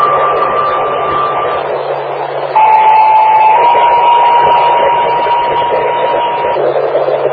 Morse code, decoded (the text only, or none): none